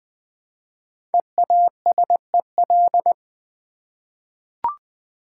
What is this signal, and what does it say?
Morse code, heard as a keyed tone, about 20 wpm, 700 Hz. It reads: EASEL
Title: Easy Words with Hints - 20wpm